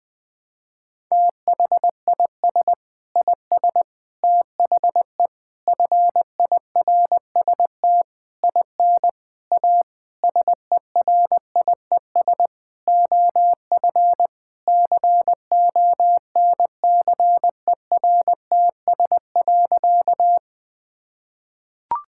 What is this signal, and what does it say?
Morse code, heard as a keyed tone, about 20 wpm, 700 Hz. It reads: THIS IS THE FIRST IN A SERIES OF CONCERTS.
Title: Sentences with 8-Word Phrases - 20wpm